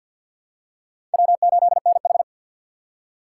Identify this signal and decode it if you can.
Morse code, heard as a keyed tone, about 51 wpm, 700 Hz. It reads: W8NH
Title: USA Call Signs - 3x Character Spacing - 50wpm